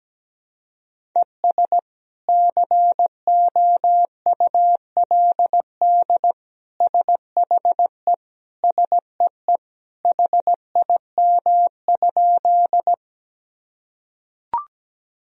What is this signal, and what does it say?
Morse code, heard as a keyed tone, about 17 wpm, 700 Hz. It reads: ES COULD SHE SEE HIM?
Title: Sentences from Top 100 Words and Abbreviations 17wpm